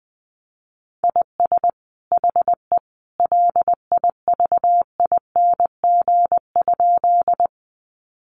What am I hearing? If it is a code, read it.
Morse code, heard as a keyed tone, about 20 wpm, 700 Hz. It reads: IS HE LIVING?
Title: Sentences from Top 1100 Words 20wpm